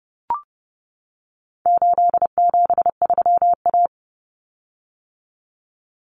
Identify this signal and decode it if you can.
Morse code, heard as a keyed tone, about 30 wpm, 700 Hz. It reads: 873A
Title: Common Cut Numbers - 4-Digit 30wpm